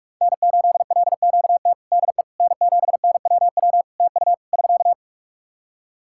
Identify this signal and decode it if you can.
Morse code, heard as a keyed tone, about 46 wpm, 700 Hz. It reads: N8LQT DE N7NWW TU <SK>
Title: Practice QSO - Exchange 4 - Ending - 45wpm